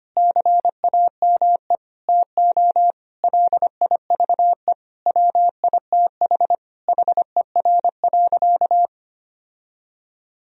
Morse code, heard as a keyed tone, about 25 wpm, 700 Hz. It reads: CAME TO LIVE WITH HER.
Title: Sentences from Top 1000 Words - Review Part 3 - 25wpm